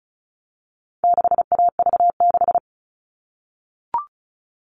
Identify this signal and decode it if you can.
Morse code, heard as a keyed tone, about 34 wpm, 700 Hz. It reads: 6AV6
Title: All Cut Numbers - 4-Digit 35wpm